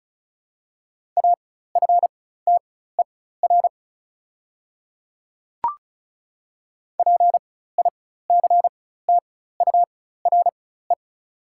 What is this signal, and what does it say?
Morse code, heard as a keyed tone, about 34 wpm, spaced as for 16 wpm, 690 Hz. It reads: AFTER PICTURE
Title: 4x Character Spacing - Top 300 Words - 35wpm